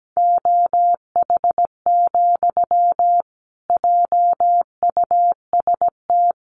Morse code, heard as a keyed tone, about 17 wpm, 700 Hz. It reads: OH, JUST
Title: Sentences from Top 500 Words 17wpm